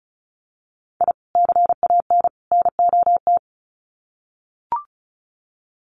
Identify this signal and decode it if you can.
Morse code, heard as a keyed tone, about 34 wpm, 700 Hz. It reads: I CAN NOT
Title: Sets of 3 Words 35wpm